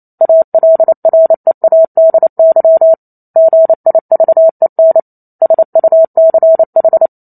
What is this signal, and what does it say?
Morse code, heard as a keyed tone, about 28 wpm, 650 Hz. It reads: ALREADY GIVEN SUCH